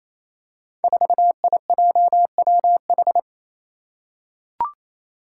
Morse code, heard as a keyed tone, about 28 wpm, 700 Hz. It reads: 4IJWH
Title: Five Challenging Letter-Number Combinations 28wpm